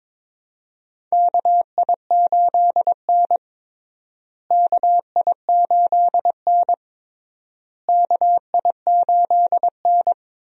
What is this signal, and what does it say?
Morse code, heard as a keyed tone, about 22 wpm, 700 Hz. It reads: KI8N KI8N KI8N
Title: USA Call Signs - 4 Characters - Repeated 3x - 22wpm